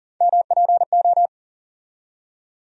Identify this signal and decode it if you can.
Morse code, heard as a keyed tone, about 38 wpm, 700 Hz. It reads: MPO
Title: USA Call Sign Postfixes - 3 Letters 40wpm